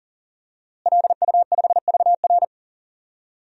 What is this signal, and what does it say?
Morse code, heard as a keyed tone, about 40 wpm, 700 Hz. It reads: LU5VR
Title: DXCC-100 Call Signs - 5 Characters - Repeated 3x - 40wpm